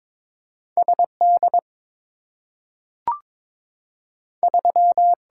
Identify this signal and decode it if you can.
Morse code, heard as a keyed tone, about 22 wpm, 700 Hz. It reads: SD 3
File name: practice / 22wpm / Two Letter-Number Combinations - 22wpm